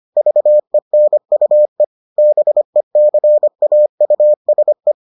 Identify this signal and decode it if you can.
Morse code, heard as a keyed tone, about 25 wpm, 600 Hz. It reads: VENUE BECAUSE